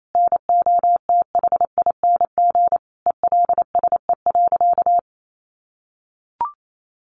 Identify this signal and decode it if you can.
Morse code, heard as a keyed tone, about 28 wpm, 700 Hz. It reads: NOTHING ELSE.
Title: Sentences from Top 1100 Words 28wpm